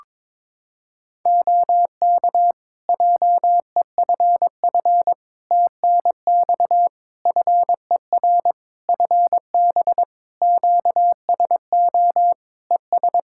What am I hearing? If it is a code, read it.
Morse code, heard as a keyed tone, about 22 wpm, 700 Hz. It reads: OK JEFF TNX FER FB QSO ES
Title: Practice QSO - Exchange 4 - Ending - 22wpm